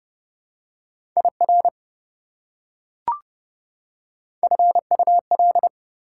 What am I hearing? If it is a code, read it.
Morse code, heard as a keyed tone, about 30 wpm, 700 Hz. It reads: IR FUL
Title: Prefixes and Suffixes 30wpm